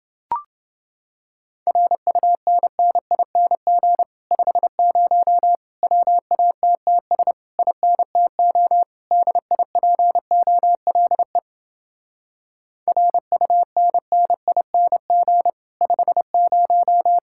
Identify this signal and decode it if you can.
Morse code, heard as a keyed tone, about 30 wpm, 700 Hz. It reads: RUNNING 50 WATTS INTO DIPOLE RUNNING 50